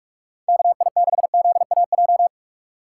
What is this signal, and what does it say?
Morse code, heard as a keyed tone, about 46 wpm, 700 Hz. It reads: KI6ZAJ
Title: USA Call Signs - 6 Characters - 4x Character Spacing - 45wpm